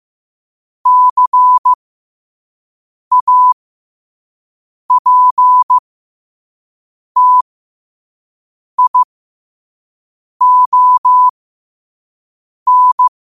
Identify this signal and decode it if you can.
Morse code, heard as a keyed tone, about 15 wpm, spaced as for 5 wpm, 1000 Hz. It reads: CAPTION